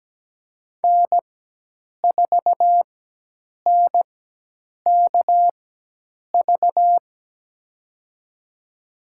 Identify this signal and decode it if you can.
Morse code, heard as a keyed tone, about 17 wpm, spaced as for 8 wpm, 700 Hz. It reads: N4NKV
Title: USA Call Signs - 4x Character Spacing - 17wpm